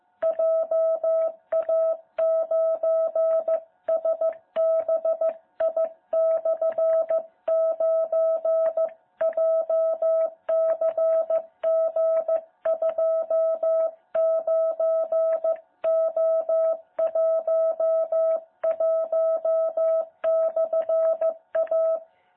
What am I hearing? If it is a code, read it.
Morse code, heard as a keyed tone, about 15 wpm, 660 Hz. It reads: JA9SBI/9 JCG29O11/A
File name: jh8cbh / 2021cw